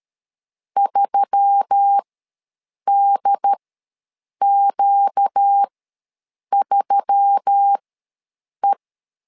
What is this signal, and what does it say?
Morse code, heard as a keyed tone, about 13 wpm, spaced as for 7 wpm, 800 Hz. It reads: 3DQ3E
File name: LEZIONI / 4 Q1S3 ws 12wpm 2038